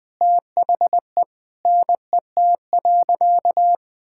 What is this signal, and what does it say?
Morse code, heard as a keyed tone, about 20 wpm, 700 Hz. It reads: THE NET.
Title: Sentences with 9-Word Phrases - 20wpm